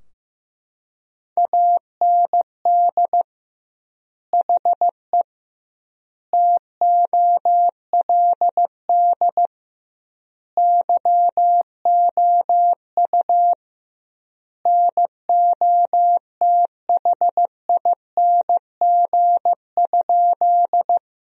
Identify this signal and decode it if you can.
Morse code, heard as a keyed tone, about 15 wpm, 700 Hz. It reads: AND HE TOLD YOU NOTHING?
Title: 2x Word-Spacing - Sentences from Top 500 Words - 15wpm